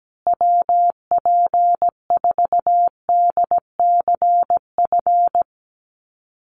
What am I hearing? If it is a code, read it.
Morse code, heard as a keyed tone, about 17 wpm, 700 Hz. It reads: WP4DCF
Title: USA Call Signs Repeated 2x 17wpm